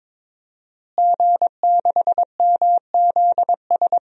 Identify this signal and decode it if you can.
Morse code, heard as a keyed tone, about 22 wpm, 700 Hz. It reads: G6MZS
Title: DXCC-100 Call Signs - 5 Characters - Repeated 2x - 22wpm